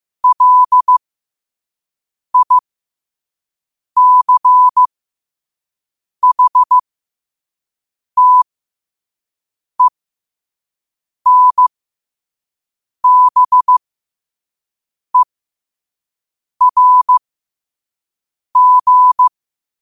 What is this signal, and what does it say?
Morse code, heard as a keyed tone, about 15 wpm, spaced as for 5 wpm, 1000 Hz. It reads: LICHTENBERG
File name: qotdfiles / 5wpm